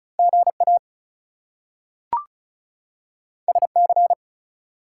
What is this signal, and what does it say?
Morse code, heard as a keyed tone, about 36 wpm, 700 Hz. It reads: GA SC
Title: US States and Canadian Provinces 35wpm